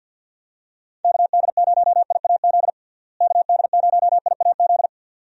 Morse code, heard as a keyed tone, about 52 wpm, 700 Hz. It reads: KD0IAZ KD0IAZ
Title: USA Call Signs - 6 Characters - Repeated 2x - 50wpm